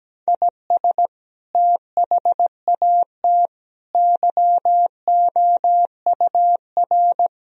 Morse code, heard as a keyed tone, about 17 wpm, 700 Hz. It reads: IS THAT YOUR
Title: Sentences from Top 1000 Words - Review Part 2 - 17wpm